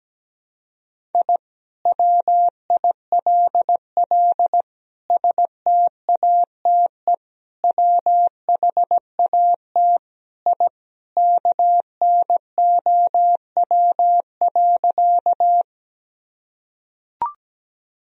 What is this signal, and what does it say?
Morse code, heard as a keyed tone, about 17 wpm, 700 Hz. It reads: I WILL STATE WHAT I KNOW.
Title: Sentences from Top 1000 Words - Review Part 2 - 17wpm